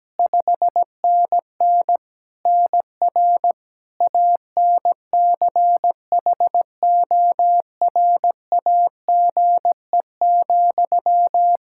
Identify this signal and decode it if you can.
Morse code, heard as a keyed tone, about 17 wpm, 700 Hz. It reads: 5NN NR ANCHORAGE,
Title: Practice QSO - Exchange 1 - Beginning - 17wpm